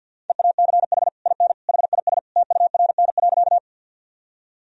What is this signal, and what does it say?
Morse code, heard as a keyed tone, about 53 wpm, 700 Hz. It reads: EACH IN HIS TURN.
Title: Sentences from Top 1000 Words - Review Part 1 - 50wpm